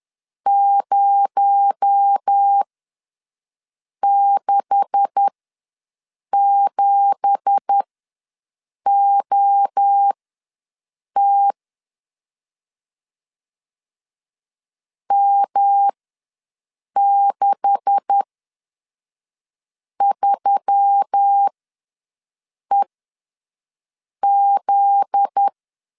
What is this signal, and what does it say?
Morse code, heard as a keyed tone, about 10 wpm, spaced as for 5 wpm, 800 Hz. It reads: 067OT M63EZ
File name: LEZIONI / 9 2867 ws 10wpm 4215